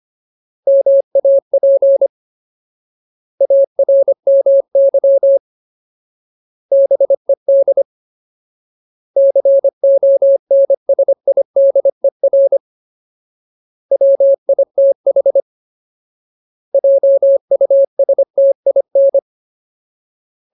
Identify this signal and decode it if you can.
Morse code, heard as a keyed tone, about 25 wpm, 550 Hz. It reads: MAP ARMY BED CONSIDER WITH JUSTIN
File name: Ultimate Word List / 25WPM Ultimate